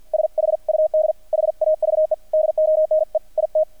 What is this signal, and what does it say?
Morse code, heard as a keyed tone, about 48 wpm, 620 Hz. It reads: SHUD HAVE DONE IT
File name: archived / CRSnet